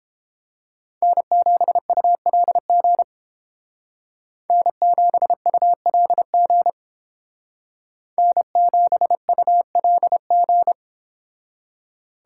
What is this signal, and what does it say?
Morse code, heard as a keyed tone, about 30 wpm, 700 Hz. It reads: N7ULG N7ULG N7ULG